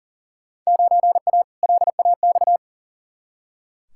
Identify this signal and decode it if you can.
Morse code, heard as a keyed tone, about 40 wpm, 700 Hz. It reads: 9A LAX
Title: Field Day Exchanges Encore - 40wpm